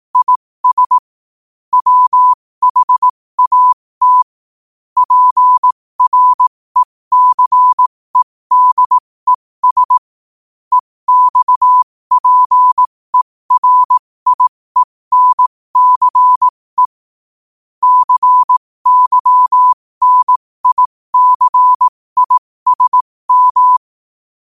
Morse code, heard as a keyed tone, about 18 wpm, spaced as for 15 wpm, 1000 Hz. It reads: IS WHAT PRECEDES EXPERIENCE CYNICISM
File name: qotdfiles / 15wpm